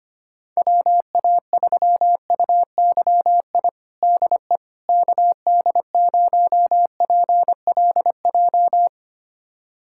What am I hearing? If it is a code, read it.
Morse code, heard as a keyed tone, about 25 wpm, 700 Hz. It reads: WA3UYI DE KD0PLJ
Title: USA Callsign DE Callsign - 6 Characters - 25wpm